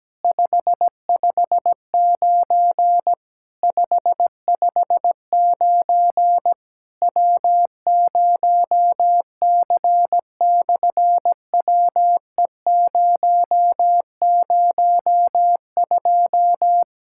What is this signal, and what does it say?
Morse code, heard as a keyed tone, about 17 wpm, 700 Hz. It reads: 559 559 W0C/WE002
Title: SOTA QSO Practice 17wpm